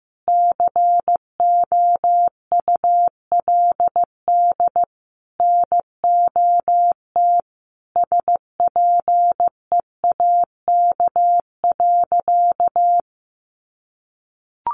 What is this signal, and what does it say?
Morse code, heard as a keyed tone, about 14 wpm, 700 Hz. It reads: COULD NOT SPEAK.
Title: Sentences from Top 1100 Words 15wpm